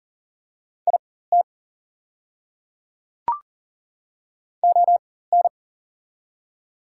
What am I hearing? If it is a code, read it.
Morse code, heard as a keyed tone, about 40 wpm, spaced as for 19 wpm, 700 Hz. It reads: IT ON
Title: 4x Character Spacing - Top 200 Words - 40wpm